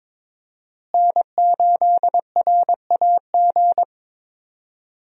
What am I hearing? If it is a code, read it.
Morse code, heard as a keyed tone, about 22 wpm, 700 Hz. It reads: N8RAG
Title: USA Call Signs - 5 Characters - Repeated 2x - 22wpm